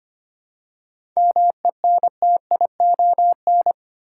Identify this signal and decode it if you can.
Morse code, heard as a keyed tone, about 25 wpm, 700 Hz. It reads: MENTION
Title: Seven Letter Words - 25wpm